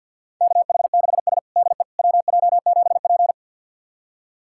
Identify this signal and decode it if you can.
Morse code, heard as a keyed tone, about 52 wpm, 700 Hz. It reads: KH6S DE WJ7P
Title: USA Callsign DE Callsign - 4 Characters - 50wpm